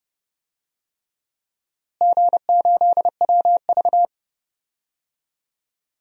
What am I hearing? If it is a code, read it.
Morse code, heard as a keyed tone, about 30 wpm, 700 Hz. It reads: G8WV
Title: DXCC-100 Call Signs - 4 Characters - Repeated 3x - 30wpm